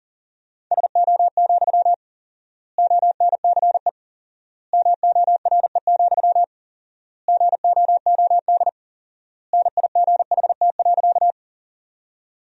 Morse code, heard as a keyed tone, about 37 wpm, 700 Hz. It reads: SO, ONCE MORE, GOOD NIGHT.
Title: 4x Word-Spacing - Sentences from Top 300 Words - 40wpm